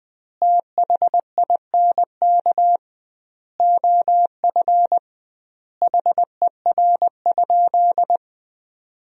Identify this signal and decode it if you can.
Morse code, heard as a keyed tone, about 20 wpm, 700 Hz. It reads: THINK OF HER?
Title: 2x Word-Spacing - Sentences from Top 200 Words - 20wpm